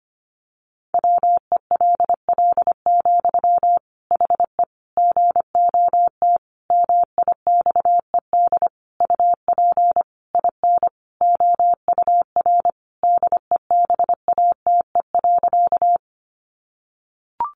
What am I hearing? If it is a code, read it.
Morse code, heard as a keyed tone, about 25 wpm, 700 Hz. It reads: WELL, HE GOT MIXED UP IN OUR DEBATE.
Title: Sentences from Top 1500 Words 25wpm